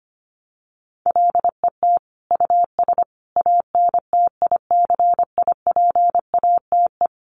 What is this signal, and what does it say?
Morse code, heard as a keyed tone, about 25 wpm, 700 Hz. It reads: LET US ANTICIPATE
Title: Sentences from Top 1500 Words 25wpm